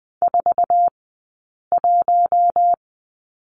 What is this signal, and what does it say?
Morse code, heard as a keyed tone, about 20 wpm, spaced as for 8 wpm, 700 Hz. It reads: 41
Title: 126 Lesson 21d - Number 6 - Two Characters 20wpm